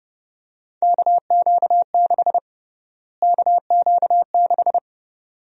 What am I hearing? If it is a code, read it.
Morse code, heard as a keyed tone, about 29 wpm, 700 Hz. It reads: KQ6 KQ6
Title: USA Call Sign Prefixes - Repeated 2x - 30wpm